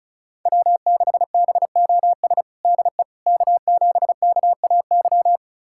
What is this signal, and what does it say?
Morse code, heard as a keyed tone, about 34 wpm, 700 Hz. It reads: W6BOS DE K7KAY
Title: USA Callsign DE Callsign - 5 Characters - 35wpm